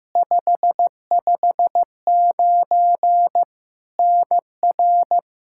Morse code, heard as a keyed tone, about 16 wpm, 700 Hz. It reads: 559 NR
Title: K1USN SST QSO Practice 15wpm